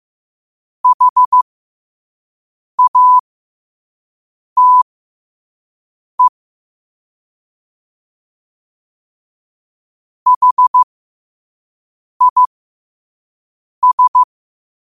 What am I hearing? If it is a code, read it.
Morse code, heard as a keyed tone, about 15 wpm, spaced as for 5 wpm, 1000 Hz. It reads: HATE HIS
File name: qotdfiles / 5wpm